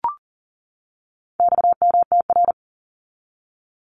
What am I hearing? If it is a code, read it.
Morse code, heard as a keyed tone, about 40 wpm, 700 Hz. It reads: XMTR
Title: Abbreviations 40wpm